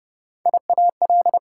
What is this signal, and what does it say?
Morse code, heard as a keyed tone, about 40 wpm, spaced as for 35 wpm, 690 Hz. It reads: IAL
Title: Speed-Racing Prefixes and Suffixes 30wpm